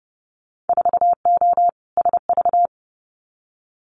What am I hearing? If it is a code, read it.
Morse code, heard as a keyed tone, about 30 wpm, 700 Hz. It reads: 4O SV
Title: Winter Field Day Exchanges 30wpm